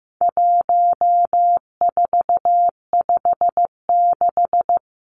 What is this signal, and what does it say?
Morse code, heard as a keyed tone, about 15 wpm, 700 Hz. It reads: 1456
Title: Numbers - 4-Digit 15wpm